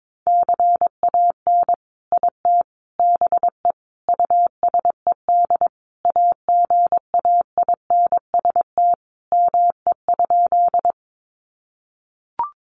Morse code, heard as a keyed tone, about 22 wpm, 700 Hz. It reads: CAN IT BE USED AGAINST ME?
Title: Sentences from Top 600 Words 22wpm